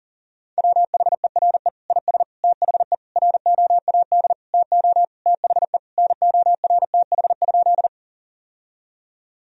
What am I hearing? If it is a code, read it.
Morse code, heard as a keyed tone, about 40 wpm, 700 Hz. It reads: WHERE IS THE ROAD TO THE NORTH?
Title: Sentences from Top 1000 Words - Review Part 2 - 40wpm